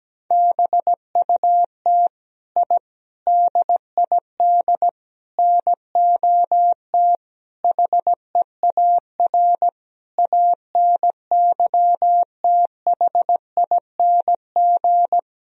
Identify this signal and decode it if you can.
Morse code, heard as a keyed tone, about 17 wpm, 700 Hz. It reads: BUT I DID NOT HEAR ANYTHING
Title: Sentences from Top 1000 Words - Review Part 4 - 17wpm